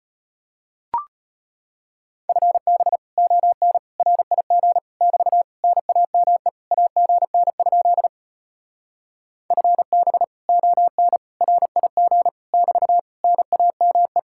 Morse code, heard as a keyed tone, about 33 wpm, 700 Hz. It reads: FB ON RIG <BT> NAME AGN? FB ON RIG <BT> NAME